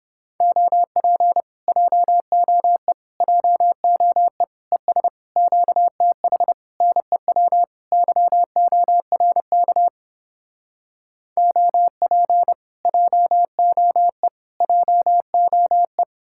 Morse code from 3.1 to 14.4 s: JOE ES QTH NEW YORK OP JOE

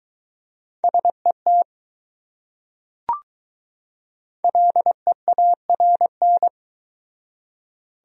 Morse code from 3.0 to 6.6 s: LEARN

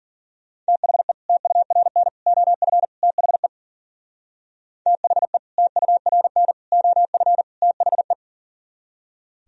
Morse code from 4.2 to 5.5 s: THE

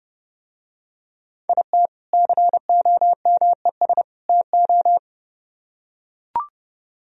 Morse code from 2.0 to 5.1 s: COMES TO

